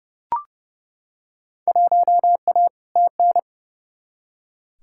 Morse code, heard as a keyed tone, about 30 wpm, 700 Hz. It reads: 1A TN
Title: Field Day Exchanges - 30wpm